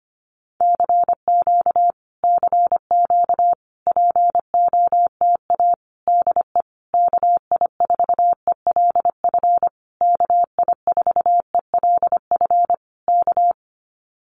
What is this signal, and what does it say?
Morse code, heard as a keyed tone, about 25 wpm, 700 Hz. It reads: CQ CQ POTA DE KI4ELF KI4ELF K
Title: Common QSO Phrases Encore 25wpm